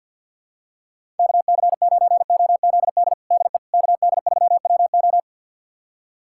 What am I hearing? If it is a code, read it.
Morse code, heard as a keyed tone, about 52 wpm, 700 Hz. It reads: KC9OZD DE KD3WO
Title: USA Callsign DE Callsign 50wpm